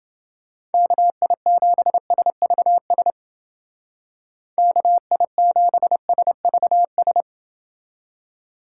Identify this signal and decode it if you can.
Morse code, heard as a keyed tone, about 30 wpm, 700 Hz. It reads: KI7SVS KI7SVS